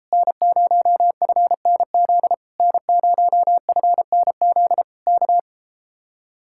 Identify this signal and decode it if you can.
Morse code, heard as a keyed tone, about 37 wpm, 700 Hz. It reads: N0FNZ N0FNZ K